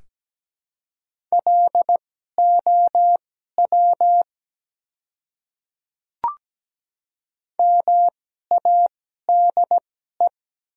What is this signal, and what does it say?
Morse code, heard as a keyed tone, about 17 wpm, spaced as for 12 wpm, 700 Hz. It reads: LOW MADE
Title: 2x Character Spacing - Top 200 Words - 17wpm